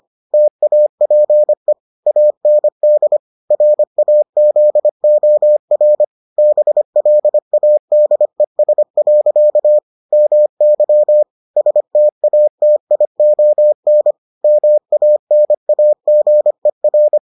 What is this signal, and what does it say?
Morse code, heard as a keyed tone, about 25 wpm, 600 Hz. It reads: TAPE AND RAZOR BLADES. MY STATION MANAGER